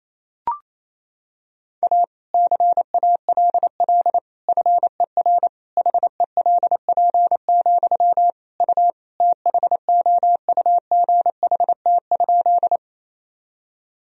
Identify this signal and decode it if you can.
Morse code, heard as a keyed tone, about 28 wpm, 700 Hz. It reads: A CALL FER HELP, U THOUGHT?
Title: Sentences from Top 300 Words and Abbreviations 28wpm